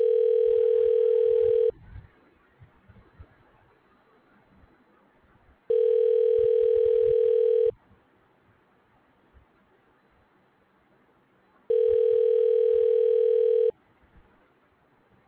Ringback tone (North America), 440 + 480 Hz.